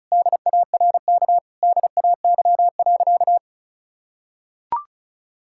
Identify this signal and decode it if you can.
Morse code, heard as a keyed tone, about 36 wpm, 700 Hz. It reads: DARK DAY.
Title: Sentences from Top 1000 Words - Review Part 5 - 35wpm